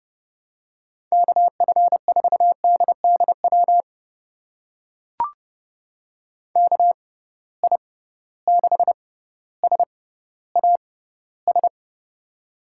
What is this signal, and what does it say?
Morse code, heard as a keyed tone, about 29 wpm, spaced as for 15 wpm, 700 Hz. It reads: KF4DDW KI6SAS
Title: USA Call Signs - 6 Characters - 6x Character Spacing - 30wpm